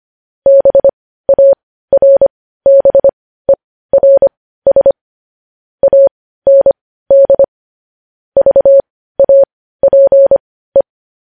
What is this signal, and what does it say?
Morse code, heard as a keyed tone, about 25 wpm, spaced as for 15 wpm, 550 Hz. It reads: BARBERS AND VAPE
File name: podcast / News headlines-1x-25x15